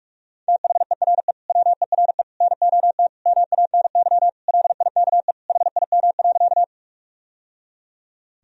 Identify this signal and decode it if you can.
Morse code, heard as a keyed tone, about 46 wpm, 700 Hz. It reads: THERE WERE NOT MANY LIKE HIM.
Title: Sentences from Top 1000 Words - Review Part 1 - 45wpm